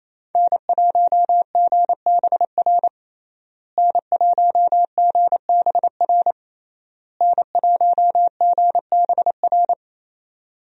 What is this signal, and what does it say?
Morse code, heard as a keyed tone, about 28 wpm, 700 Hz. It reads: N1GBR N1GBR N1GBR
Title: USA Call Signs - 5 Characters - Repeated 3x - 28wpm